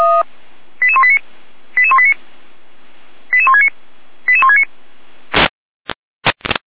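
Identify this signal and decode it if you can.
Five-tone selective call (ZVEI), heard as quick runs of five tones, tones of about 80 ms. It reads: 88189 88189 88169 88169
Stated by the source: SDR reception of ZVEI five-tone call 88166.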